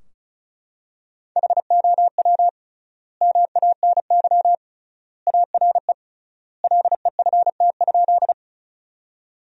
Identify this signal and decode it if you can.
Morse code, heard as a keyed tone, about 36 wpm, 700 Hz. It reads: HOW MANY ARE LEFT?
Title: 3x Word-Spacing - Sentences from Top 300 Words - 35wpm